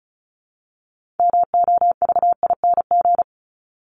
Morse code, heard as a keyed tone, about 34 wpm, 700 Hz. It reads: MOVING
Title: ING Suffix - Top 100 Words with Base Words 35wpm